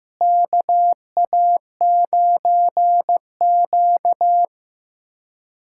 Morse code, heard as a keyed tone, about 15 wpm, 700 Hz. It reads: KA9Q
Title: USA Callsign DE Callsign - 4 Characters - 15wpm